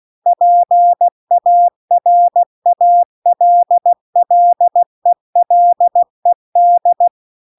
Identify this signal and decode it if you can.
Morse code, heard as a keyed tone, about 16 wpm, 700 Hz. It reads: PARALLELED